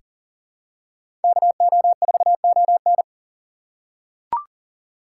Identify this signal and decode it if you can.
Morse code, heard as a keyed tone, about 40 wpm, 700 Hz. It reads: KO4ON
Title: USA Call Signs - 6x Character Spacing - 40wpm